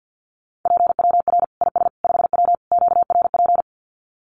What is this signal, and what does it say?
Morse code, heard as a keyed tone, about 50 wpm, 700 Hz. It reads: PWR IS 5W QRP
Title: Common QSO Phrases 50wpm